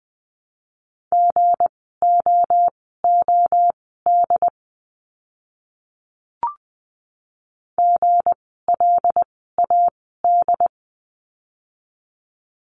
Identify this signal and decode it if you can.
Morse code, heard as a keyed tone, about 20 wpm, spaced as for 14 wpm, 700 Hz. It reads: GOOD GLAD
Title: 160 Lesson 26f - Letter g - Words 20wpm